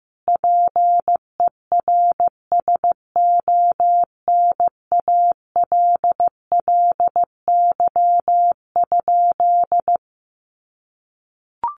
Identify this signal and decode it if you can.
Morse code, heard as a keyed tone, about 15 wpm, 700 Hz. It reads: PERSONALLY?
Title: Sentences from Top 1400 Words 15wpm